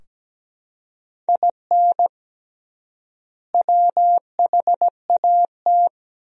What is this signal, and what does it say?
Morse code, heard as a keyed tone, about 17 wpm, 700 Hz. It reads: IN WHAT
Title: 3x Word-Spacing - Sentences from Top 200 Words - 17wpm